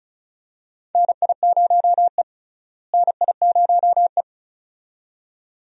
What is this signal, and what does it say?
Morse code, heard as a keyed tone, about 34 wpm, 700 Hz. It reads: NI0E NI0E
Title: USA Call Signs - 4 Characters - Repeated 2x - 35wpm